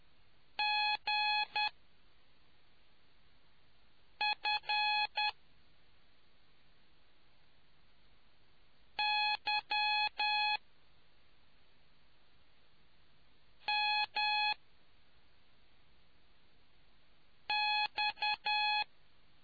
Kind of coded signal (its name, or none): Morse code